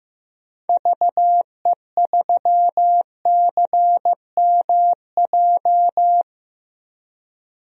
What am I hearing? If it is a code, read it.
Morse code, heard as a keyed tone, about 15 wpm, 700 Hz. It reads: VE3CMJ
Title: DXCC-100 Call Signs - 6 Characters - 15wpm